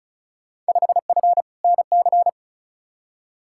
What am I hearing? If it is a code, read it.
Morse code, heard as a keyed tone, about 36 wpm, 690 Hz. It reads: 5F NC
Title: Field Day Exchanges Encore - 35wpm